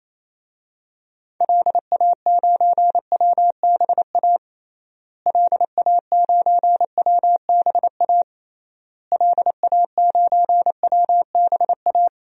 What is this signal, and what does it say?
Morse code, heard as a keyed tone, about 28 wpm, 700 Hz. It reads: LA9WBA LA9WBA LA9WBA
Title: DXCC-100 Call Signs - 6 Characters - Repeated 3x - 28wpm